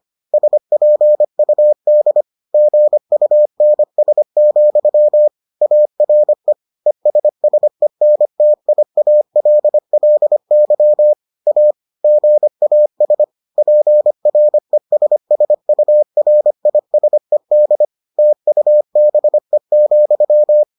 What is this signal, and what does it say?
Morse code, heard as a keyed tone, about 25 wpm, 600 Hz. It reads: SPUD GUNS, ARE ESSENTIALLY A GAS PRESSURISED TUBE,